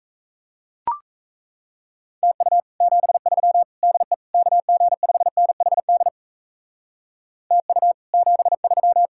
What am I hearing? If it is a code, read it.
Morse code, heard as a keyed tone, about 34 wpm, 700 Hz. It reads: TU 73 DE KG5NHD TU 73
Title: Speed-Racing Common QSO Phrases - 28wpm